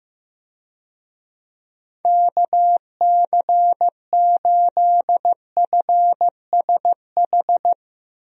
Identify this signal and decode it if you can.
Morse code, heard as a keyed tone, about 15 wpm, 700 Hz. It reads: KC8FSH
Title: USA Call Signs - 6 Characters - Repeated 2x - 15wpm